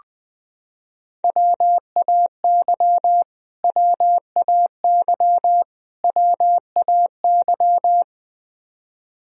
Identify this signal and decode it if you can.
Morse code, heard as a keyed tone, about 20 wpm, 700 Hz. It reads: WAY WAY WAY
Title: Top 100 Words Repeated 20wpm